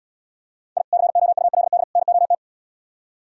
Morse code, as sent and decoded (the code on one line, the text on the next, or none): .. / -.-. --- ..- .-.. -.. / -. --- -
I COULD NOT